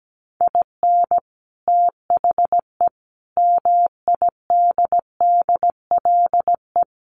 Morse code, heard as a keyed tone, about 17 wpm, 700 Hz. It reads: IN THE MIDDLE
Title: Sets of 5 Words 17wpm